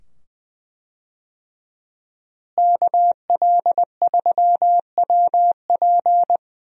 Morse code, heard as a keyed tone, about 20 wpm, 700 Hz. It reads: KL3WP
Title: USA Call Signs - 5 Characters - Repeated 2x - 20wpm